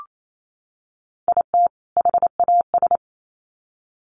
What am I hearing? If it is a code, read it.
Morse code, heard as a keyed tone, about 28 wpm, 700 Hz. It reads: IT HAS
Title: Sets of 2 Words 28wpm